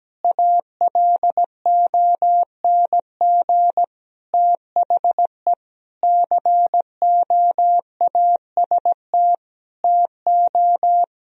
Morse code, heard as a keyed tone, about 17 wpm, 700 Hz. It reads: ALONG THE COAST TO
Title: Sentences from Top 1000 Words - Review Part 4 - 17wpm